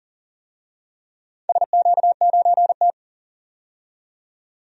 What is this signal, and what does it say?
Morse code, heard as a keyed tone, about 38 wpm, 700 Hz. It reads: SQ9T
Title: DXCC-100 Call Signs - 4 Characters - 40wpm